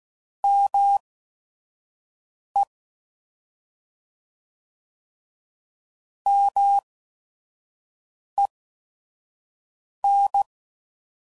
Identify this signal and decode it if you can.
Morse code, heard as a keyed tone, about 16 wpm, spaced as for 5 wpm, 800 Hz. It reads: ME MEN